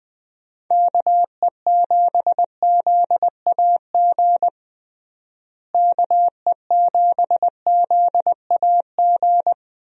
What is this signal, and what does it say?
Morse code, heard as a keyed tone, about 20 wpm, 700 Hz. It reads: KE7ZAG KE7ZAG
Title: USA Call Signs - 6 Characters - Repeated 3x - 20wpm